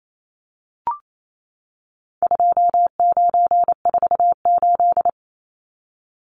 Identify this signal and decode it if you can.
Morse code, heard as a keyed tone, about 28 wpm, 700 Hz. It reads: 2948
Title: Numbers - 4-Digit 28wpm